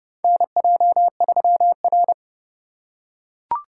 Morse code, heard as a keyed tone, about 30 wpm, 700 Hz. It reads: NJ3R
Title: USA Call Signs - 4 Characters - 4x Character Spacing - 30wpm